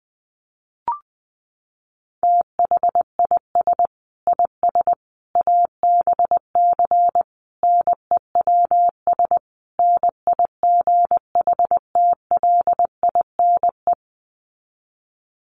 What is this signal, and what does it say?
Morse code, heard as a keyed tone, about 20 wpm, 700 Hz. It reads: THIS IS ABC NEWS NIGHTLINE
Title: Sets of 5 Words 20wpm